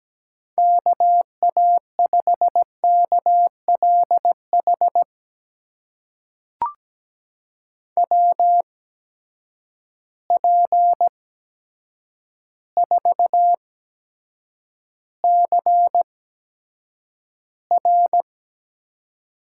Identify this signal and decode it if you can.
Morse code, heard as a keyed tone, about 17 wpm, spaced as for 8 wpm, 700 Hz. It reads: KA5KLH WP4CR
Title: USA Call Signs - 6 Characters - 8x Character Spacing - 17wpm